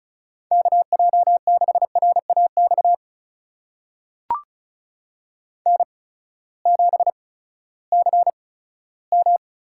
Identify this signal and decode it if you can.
Morse code, heard as a keyed tone, about 36 wpm, spaced as for 18 wpm, 700 Hz. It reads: KJ6RAX N7CM
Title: USA Call Signs - 8x Character Spacing - 35wpm